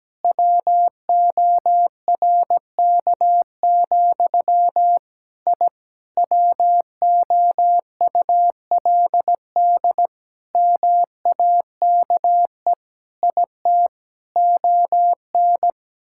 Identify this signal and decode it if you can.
Morse code, heard as a keyed tone, about 17 wpm, 700 Hz. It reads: WORK, I WOULD MAKE IT ON
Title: Sentences from Top 1000 Words - Review Part 6 - 17wpm